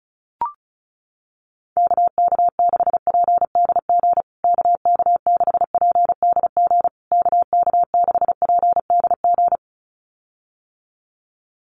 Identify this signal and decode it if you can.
Morse code, heard as a keyed tone, about 34 wpm, 700 Hz. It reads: KK6PDG KK6PDG KK6PDG
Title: USA Call Signs Repeated 3x 35wpm